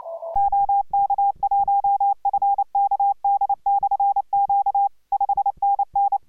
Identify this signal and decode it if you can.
Morse code, heard as a keyed tone, about 29 wpm, 810 Hz. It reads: OK1FKD/Q 5NN